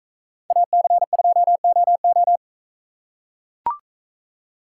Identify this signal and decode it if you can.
Morse code, heard as a keyed tone, about 40 wpm, 700 Hz. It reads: AC2OO